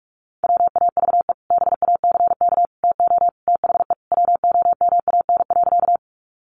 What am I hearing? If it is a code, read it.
Morse code, heard as a keyed tone, about 44 wpm, 700 Hz. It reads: WAVE BACK TO THE WOMAN.
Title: Sentences from Top 1400 Words 45wpm